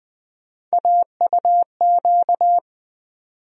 Morse code, heard as a keyed tone, about 20 wpm, 700 Hz. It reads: AUQ
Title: USA Call Sign Postfixes - 3 Letters 20wpm